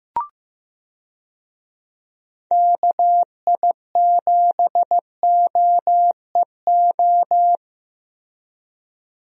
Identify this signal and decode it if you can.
Morse code, heard as a keyed tone, about 15 wpm, 700 Hz. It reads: KI7OEO
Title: USA Call Signs - 6 Characters - 15wpm